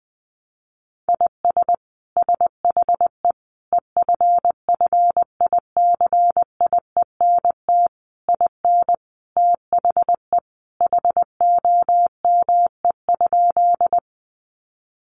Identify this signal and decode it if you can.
Morse code, heard as a keyed tone, about 20 wpm, 700 Hz. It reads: IS SHE EFFICIENT IN THE HOME?
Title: Sentences from Top 1300 Words 20wpm